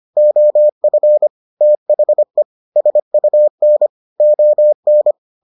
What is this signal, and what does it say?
Morse code, heard as a keyed tone, about 25 wpm, 600 Hz. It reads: OF THE SUN ON